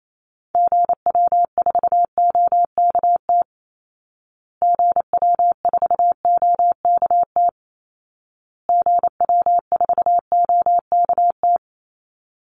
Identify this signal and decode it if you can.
Morse code, heard as a keyed tone, about 28 wpm, 700 Hz. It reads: GW4OKT GW4OKT GW4OKT
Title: Call Signs Repeated 2x - 28wpm